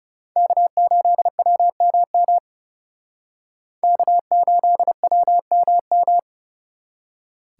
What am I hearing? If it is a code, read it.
Morse code, heard as a keyed tone, about 37 wpm, 700 Hz. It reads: K8WMM K8WMM
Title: Speed Racing - USA 1x3 Call Signs 30wpm